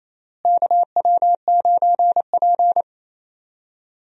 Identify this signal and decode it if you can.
Morse code, heard as a keyed tone, about 28 wpm, 700 Hz. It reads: KW9P
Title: USA Call Signs - 4 Characters - Repeated 3x - 28wpm